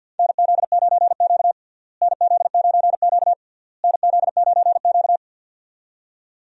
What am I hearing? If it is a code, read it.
Morse code, heard as a keyed tone, about 51 wpm, 700 Hz. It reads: NZ9Q NZ9Q NZ9Q
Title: USA Call Signs - 4 Characters - Repeated 3x - 50wpm